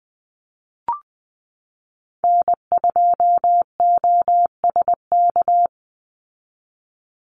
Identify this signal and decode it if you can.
Morse code, heard as a keyed tone, about 20 wpm, 700 Hz. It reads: N2OSK
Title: USA Call Signs 20wpm